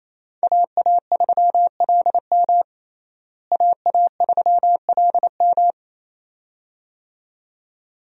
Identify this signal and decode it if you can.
Morse code, heard as a keyed tone, about 28 wpm, 700 Hz. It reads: AA3LM AA3LM
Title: USA Call Signs - 5 Characters - Repeated 2x - 28wpm